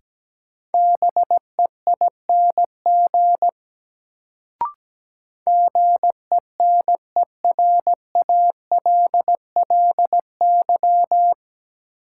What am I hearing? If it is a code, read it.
Morse code, heard as a keyed tone, about 17 wpm, 700 Hz. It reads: BEING GENERALLY
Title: Sentences from Top 600 Words - New Word Drill - 17wpm